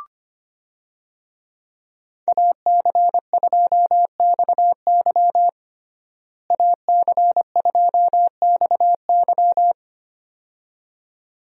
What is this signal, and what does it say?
Morse code, heard as a keyed tone, about 25 wpm, 700 Hz. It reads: AC2XY AC2XY
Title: USA Call Signs - 5 Characters - Repeated 2x - 25wpm